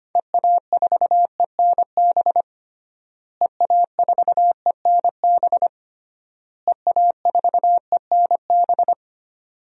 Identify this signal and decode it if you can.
Morse code, heard as a keyed tone, about 25 wpm, 700 Hz. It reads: EA4ENB EA4ENB EA4ENB
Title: DXCC-100 Call Signs - 6 Characters - Repeated 3x - 25wpm